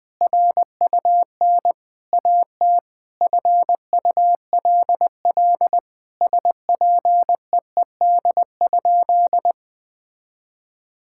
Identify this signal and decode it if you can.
Morse code, heard as a keyed tone, about 20 wpm, 700 Hz. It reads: RUN AT FULL SPEED?
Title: Sentences from Top 1000 Words - Review Part 5 - 20wpm